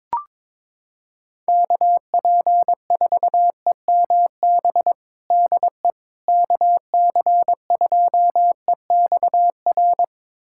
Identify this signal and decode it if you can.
Morse code, heard as a keyed tone, about 22 wpm, 700 Hz. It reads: KP4EMB DE KC2EXR
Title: USA Callsign DE Callsign 22wpm